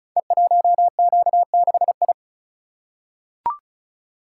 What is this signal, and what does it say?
Morse code, heard as a keyed tone, about 36 wpm, 700 Hz. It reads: E1Q6I
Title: Code Groups - 5 Letter:Numbers - 35wpm